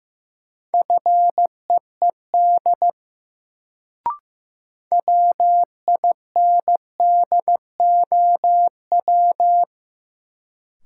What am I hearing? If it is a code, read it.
Morse code, heard as a keyed tone, about 15 wpm, 700 Hz. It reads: FEED WINDOW
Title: Sentences from Top 1600 Words - New Word Drill - 15wpm